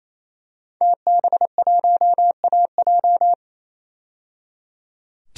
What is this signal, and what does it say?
Morse code, heard as a keyed tone, about 28 wpm, 700 Hz. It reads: TB1AJ
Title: DXCC-100 Call Signs - 5 Characters - 28wpm